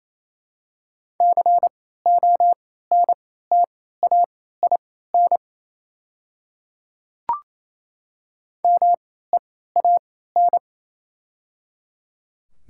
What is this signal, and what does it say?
Morse code, heard as a keyed tone, about 28 wpm, spaced as for 16 wpm, 700 Hz. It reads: CONTAIN MEAN
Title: 3x Character Spacing - Top 500 Words - 28wpm